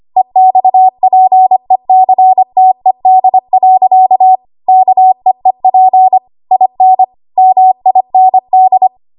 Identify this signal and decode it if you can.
Morse code, heard as a keyed tone, about 25 wpm, 750 Hz. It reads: EXPECTED. KEEP IN MIND
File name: Archive / 25 WPM / 25WPM